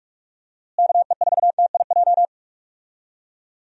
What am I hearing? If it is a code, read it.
Morse code, heard as a keyed tone, about 46 wpm, 700 Hz. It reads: KE4TIJ
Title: USA Call Signs - 6 Characters - Repeated 2x - 45wpm